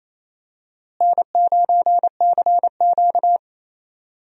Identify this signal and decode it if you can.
Morse code, heard as a keyed tone, about 28 wpm, 700 Hz. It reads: N9CQ
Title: USA Call Signs - 4 Characters - Repeated 2x - 28wpm